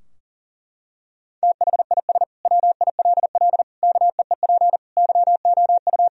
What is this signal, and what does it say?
Morse code, heard as a keyed tone, about 37 wpm, 700 Hz. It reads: THIS WILL KEEP YOU
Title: Sentences from Top 1000 Words - Review Part 2 - 40wpm